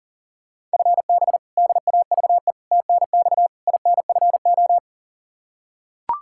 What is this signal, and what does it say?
Morse code, heard as a keyed tone, about 40 wpm, 700 Hz. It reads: FB DAVE TNX INFO
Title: K1USN SST QSO Practice 40wpm